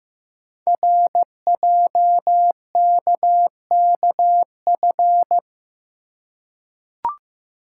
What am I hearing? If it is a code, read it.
Morse code, heard as a keyed tone, about 15 wpm, 700 Hz. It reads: RJKKF
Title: Code Groups - 5 Letters - 15wpm